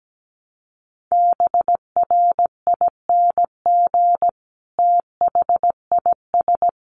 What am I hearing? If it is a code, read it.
Morse code, heard as a keyed tone, about 17 wpm, 700 Hz. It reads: BRING THIS
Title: Sentences from Top 500 Words 17wpm